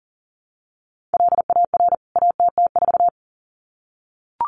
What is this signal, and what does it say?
Morse code, heard as a keyed tone, about 40 wpm, 700 Hz. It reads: LAR ATT4
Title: CWT Exchange 40wpm